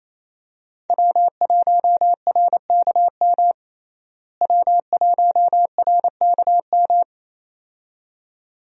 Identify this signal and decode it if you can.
Morse code, heard as a keyed tone, about 28 wpm, 700 Hz. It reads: W1RKM W1RKM
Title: USA Call Signs - 5 Characters - Repeated 2x - 28wpm